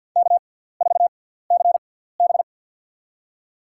Morse code, heard as a keyed tone, about 52 wpm, spaced as for 17 wpm, 700 Hz. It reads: K4CB
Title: USA Call Signs - 4 Characters - 6x Character Spacing - 50wpm